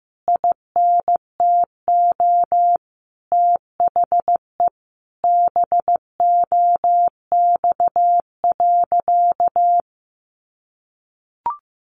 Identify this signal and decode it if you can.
Morse code, heard as a keyed tone, about 15 wpm, 700 Hz. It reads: INTO THE BOX.
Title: Sentences from Top 1000 Words 15wpm